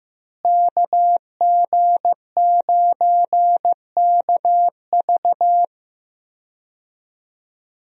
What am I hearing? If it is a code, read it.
Morse code, heard as a keyed tone, about 15 wpm, 700 Hz. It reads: KG9KV
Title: USA Call Signs - 5 Characters - Repeated 2x - 15wpm